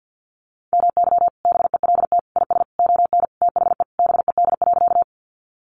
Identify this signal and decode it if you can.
Morse code, heard as a keyed tone, about 50 wpm, 700 Hz. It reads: MY BELT IS ON THE BED.
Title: Sentences from Top 1400 Words 50wpm